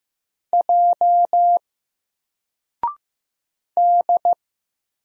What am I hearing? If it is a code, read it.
Morse code, heard as a keyed tone, about 15 wpm, 700 Hz. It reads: J D
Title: Single Letter-Number Alt 15wpm